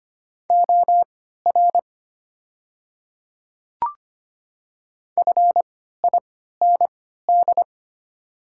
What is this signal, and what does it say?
Morse code, heard as a keyed tone, about 25 wpm, spaced as for 14 wpm, 700 Hz. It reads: OR FIND
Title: 3x Character Spacing - Top 500 Words - 25wpm